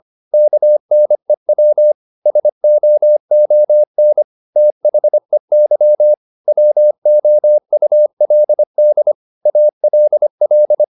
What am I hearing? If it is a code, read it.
Morse code, heard as a keyed tone, about 25 wpm, 600 Hz. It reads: KNEW SOON THEY WOULD ALL